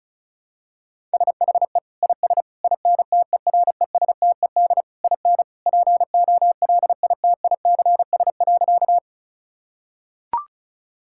Morse code, heard as a keyed tone, about 36 wpm, 700 Hz. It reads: SHE IS INTERESTED IN POLITICS.
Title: Sentences from Top 1000 Words - Review Part 6 - 35wpm